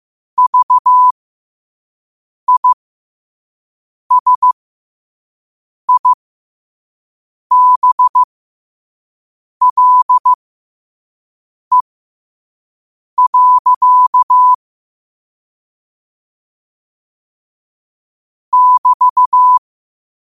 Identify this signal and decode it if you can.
Morse code, heard as a keyed tone, about 15 wpm, spaced as for 5 wpm, 1000 Hz. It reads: VISIBLE. <BT>
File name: qotdfiles / 5wpm